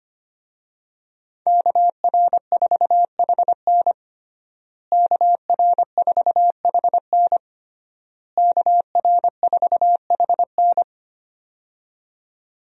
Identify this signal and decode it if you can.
Morse code, heard as a keyed tone, about 25 wpm, 700 Hz. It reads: KR4HN KR4HN KR4HN
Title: USA Call Signs - 5 Characters - Repeated 3x - 25wpm